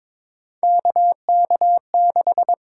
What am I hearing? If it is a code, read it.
Morse code, heard as a keyed tone, about 22 wpm, 700 Hz. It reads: KK6
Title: USA Call Signs - 8x Character Spacing - 22wpm